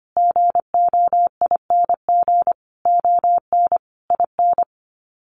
Morse code, heard as a keyed tone, about 25 wpm, 700 Hz. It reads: GOING ON IN
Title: Sets of 5 Words Encore 25wpm